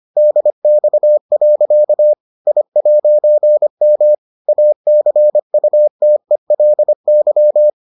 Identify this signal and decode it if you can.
Morse code, heard as a keyed tone, about 25 wpm, 600 Hz. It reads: DX. I'M ACUTELY